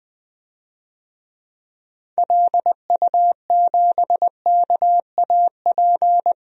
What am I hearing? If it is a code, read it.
Morse code, heard as a keyed tone, about 20 wpm, 700 Hz. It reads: LU7KAP
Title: DXCC-100 Call Signs - 6 Characters - 20wpm